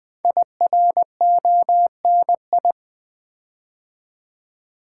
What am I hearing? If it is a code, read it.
Morse code, heard as a keyed tone, about 20 wpm, 700 Hz. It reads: IRONI
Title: Build A Word - 5 Letters At A Time - 10-Letter Words - 20wpm